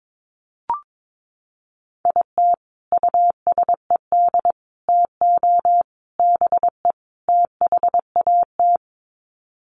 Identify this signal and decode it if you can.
Morse code, heard as a keyed tone, about 22 wpm, 700 Hz. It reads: IT USED TO BE THAT
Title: Sets of 5 Words Encore2 22wpm